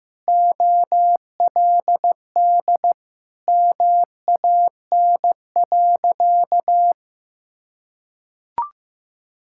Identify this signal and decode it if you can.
Morse code, heard as a keyed tone, about 16 wpm, 700 Hz. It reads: OLD MAN.
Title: Sentences from Top 1600 Words 15wpm